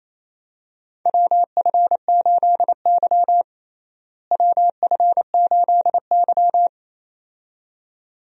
Morse code, heard as a keyed tone, about 28 wpm, 700 Hz. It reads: WF8Y WF8Y
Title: USA Call Signs - 4 Characters - Repeated 2x - 28wpm